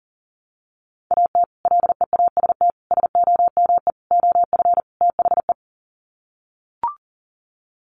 Morse code, heard as a keyed tone, about 37 wpm, 700 Hz. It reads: AT LEAST SOME OF THE
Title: Sets of 5 Words Encore 40wpm